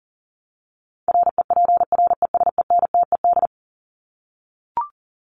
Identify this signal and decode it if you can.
Morse code, heard as a keyed tone, about 40 wpm, 700 Hz. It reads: REPRESENTED
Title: ED Suffix - Top 100 Words with Base Words 40wpm